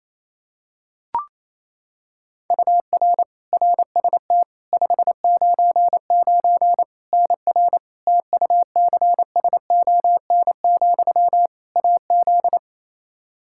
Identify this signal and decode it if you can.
Morse code, heard as a keyed tone, about 28 wpm, 700 Hz. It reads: UR RST 599 NR TUCSON, AZ
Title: Practice QSO - Exchange 1 - Beginning - 28wpm